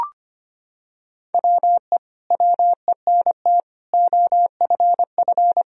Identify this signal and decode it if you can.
Morse code, heard as a keyed tone, about 25 wpm, 700 Hz. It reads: WE WENT OFF
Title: Sentences from Top 1000 Words - Review Part 3 - 25wpm